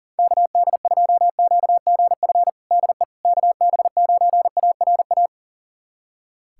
Morse code, heard as a keyed tone, about 40 wpm, 700 Hz. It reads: KD2QGF DE KB9ARA
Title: USA Callsign DE Callsign - 6 Characters - 40wpm